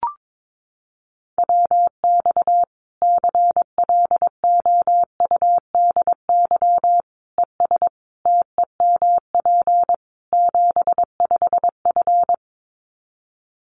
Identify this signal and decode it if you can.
Morse code, heard as a keyed tone, about 22 wpm, 700 Hz. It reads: WX CLOUDY ES TEMP 75F
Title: Common QSO Phrases 22wpm